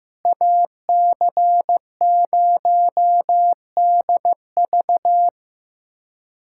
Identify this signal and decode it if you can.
Morse code, heard as a keyed tone, about 15 wpm, 700 Hz. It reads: AC0DV
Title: USA Callsign DE Callsign 15wpm